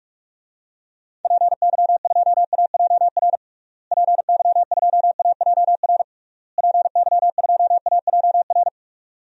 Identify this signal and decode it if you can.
Morse code, heard as a keyed tone, about 46 wpm, 700 Hz. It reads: PY2AJR PY2AJR PY2AJR
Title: DXCC-100 Call Signs - 6 Characters - Repeated 3x - 45wpm